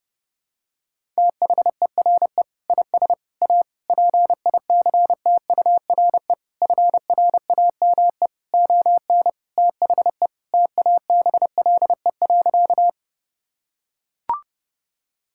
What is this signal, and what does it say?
Morse code, heard as a keyed tone, about 32 wpm, 700 Hz. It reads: THERE IS A PICTURE FRAME ON THE TABLE.
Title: Sentences from Top 1000 Words - Review Part 6 - 30wpm